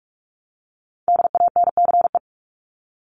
Morse code, heard as a keyed tone, about 44 wpm, 700 Hz. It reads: DANCE
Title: Top 501-750 Words 45wpm